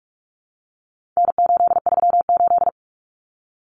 Morse code, heard as a keyed tone, about 44 wpm, 700 Hz. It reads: N838
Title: Common Cut Numbers - 4-Digit 45wpm